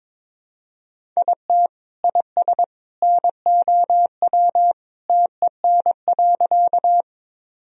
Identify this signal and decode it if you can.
Morse code, heard as a keyed tone, about 22 wpm, 700 Hz. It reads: IT IS NOW TEN.
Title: Sentences from Top 1000 Words - Review Part 2 - 22wpm